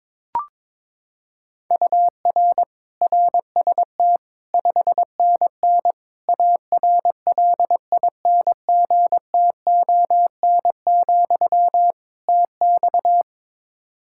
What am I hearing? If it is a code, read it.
Morse code, heard as a keyed tone, about 22 wpm, 700 Hz. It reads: UR RST 5NN ARLINGTON, TX
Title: Practice QSO - Exchange 1 - Beginning - 22wpm